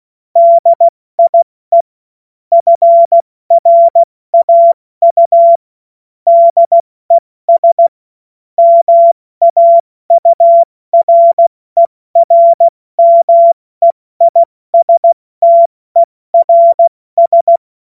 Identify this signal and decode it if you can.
Morse code, heard as a keyed tone, about 16 wpm, 680 Hz. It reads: DIE FRAU DES MAURERMEISTERS
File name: Morsekurs / Witze / Witz 17.Oktober 2023 70 F80